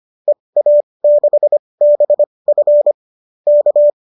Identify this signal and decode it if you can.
Morse code, heard as a keyed tone, about 25 wpm, spaced as for 20 wpm, 570 Hz. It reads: EA6BF K